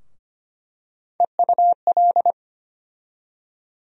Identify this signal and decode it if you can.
Morse code, heard as a keyed tone, about 25 wpm, 700 Hz. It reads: EUL
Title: USA Call Sign Postfixes - 3 Letters 25wpm